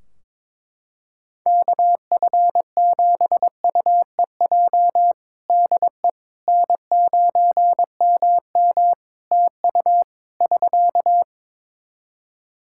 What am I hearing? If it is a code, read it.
Morse code, heard as a keyed tone, about 22 wpm, 700 Hz. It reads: KF7UEJ DE N9MM TU <SK>
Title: Practice QSO - Exchange 4 - Ending - 22wpm